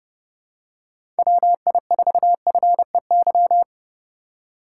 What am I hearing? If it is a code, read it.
Morse code, heard as a keyed tone, about 30 wpm, 700 Hz. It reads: WI4FEY
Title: USA Call Signs - 6 Characters - 3x Character Spacing - 30wpm